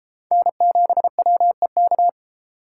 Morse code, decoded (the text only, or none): N7WEK